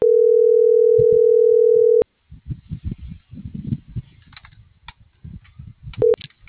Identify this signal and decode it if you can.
Ringback tone (North America), 440 + 480 Hz.